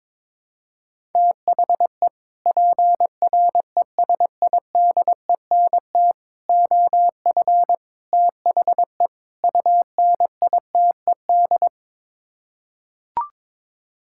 Morse code, heard as a keyed tone, about 21 wpm, 700 Hz. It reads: THE PRESIDENT OF THE UNITED
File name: practice / 22wpm / Build A Phrase - 5-Word Phrases - 22wpm